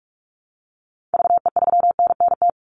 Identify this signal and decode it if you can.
Morse code, heard as a keyed tone, about 44 wpm, 700 Hz. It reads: VE3NNT
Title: Call Signs Repeated 3x - 45wpm